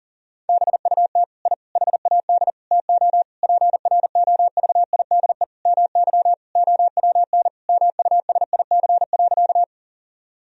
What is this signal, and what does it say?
Morse code, heard as a keyed tone, about 40 wpm, 700 Hz. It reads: BUT I HAD TO PROVIDE MY OWN MUSIC.
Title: Sentences from Top 1000 Words - Review Part 3 - 40wpm